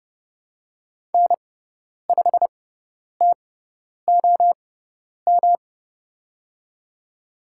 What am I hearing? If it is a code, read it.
Morse code, heard as a keyed tone, about 30 wpm, spaced as for 10 wpm, 700 Hz. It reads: N5TOM
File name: Lesson 18g - Letter m - Call Signs 30wpm